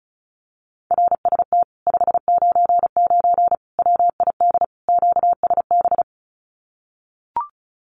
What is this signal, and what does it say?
Morse code, heard as a keyed tone, about 34 wpm, 700 Hz. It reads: RST 599 WID QSB
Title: Common QSO Phrases Encore 35wpm